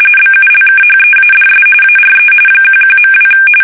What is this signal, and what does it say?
RTTY, about 45.45 baud. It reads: BULKHEADS ARE 11.2
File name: fsk850 / battleshipnj